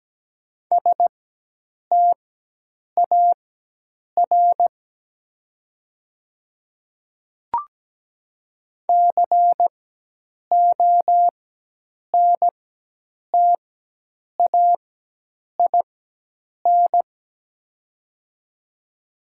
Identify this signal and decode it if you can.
Morse code, heard as a keyed tone, about 17 wpm, spaced as for 8 wpm, 700 Hz. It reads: STAR CONTAIN
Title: 4x Character Spacing - Top 500 Words - 17wpm